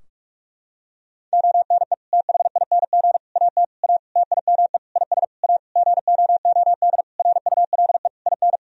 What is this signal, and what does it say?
Morse code, heard as a keyed tone, about 41 wpm, 700 Hz. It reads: ONE THING AT A TIME IS A GOOD RULE IN
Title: Sentences from Top 1000 Words - Review Part 2 - 45wpm